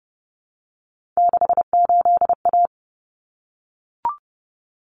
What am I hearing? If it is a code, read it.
Morse code, heard as a keyed tone, about 30 wpm, 700 Hz. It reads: 68A
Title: Common Cut Numbers - 3-Digit 30wpm